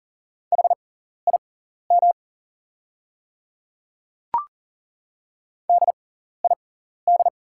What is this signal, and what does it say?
Morse code, heard as a keyed tone, about 40 wpm, spaced as for 14 wpm, 690 Hz. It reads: HIM DID
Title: 6x Character Spacing - Top 100 Words - 40wpm